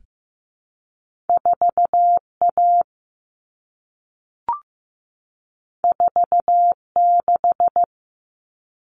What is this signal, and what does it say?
Morse code, heard as a keyed tone, about 15 wpm, 700 Hz. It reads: 4A 46
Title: All Cut Numbers - 2-Digit 15wpm